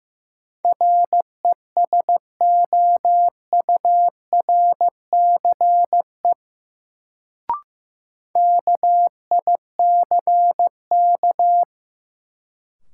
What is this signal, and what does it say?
Morse code, heard as a keyed tone, about 15 wpm, 700 Hz. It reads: RESOURCE KICK
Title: Sentences from Top 1300 Words - New Word Drill Alt - 15wpm